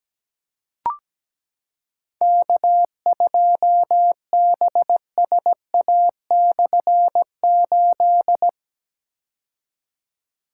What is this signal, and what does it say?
Morse code, heard as a keyed tone, about 17 wpm, 700 Hz. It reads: K2BSA/8
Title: K1USN SST QSO Practice - 17wpm